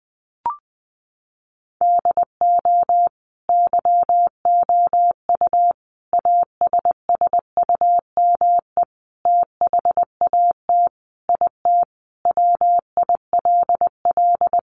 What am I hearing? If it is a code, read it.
Morse code, heard as a keyed tone, about 20 wpm, 700 Hz. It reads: DO YOU ASSUME THAT IT WILL
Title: Sentences from Top 1500 Words 20wpm